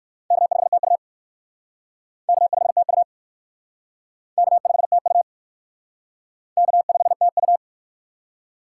Morse code, heard as a keyed tone, about 62 wpm, 700 Hz. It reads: K5TU K5TU K5TU K5TU